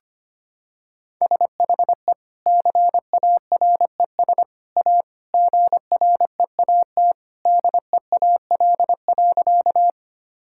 Morse code, heard as a keyed tone, about 25 wpm, 700 Hz. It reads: SHE CARES A GREAT DEAL.
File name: practice / 25wpm / Sentences with 3-Word Phrases - 25wpm